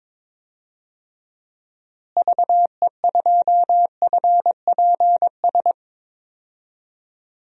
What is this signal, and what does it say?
Morse code, heard as a keyed tone, about 22 wpm, 700 Hz. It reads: VE2FPS
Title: DXCC-100 Call Signs - 6 Characters - Repeated 3x - 22wpm